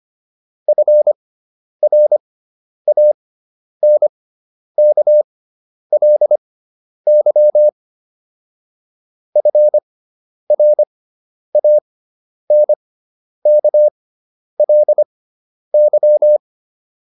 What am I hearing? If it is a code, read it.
Morse code, heard as a keyed tone, about 25 wpm, spaced as for 10 wpm, 600 Hz. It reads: FRANKLY FRANKLY
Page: fundamental-practice-files